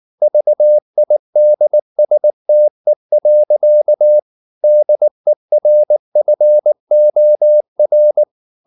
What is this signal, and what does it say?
Morse code, heard as a keyed tone, about 19 wpm, 600 Hz. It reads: VIDSTE. DERFOR
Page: Ferie 19 wpm